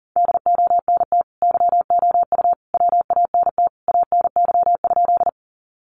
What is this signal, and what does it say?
Morse code, heard as a keyed tone, about 39 wpm, 700 Hz. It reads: DONT YOU WANT ANY?
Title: Sentences from Top 300 Words 40wpm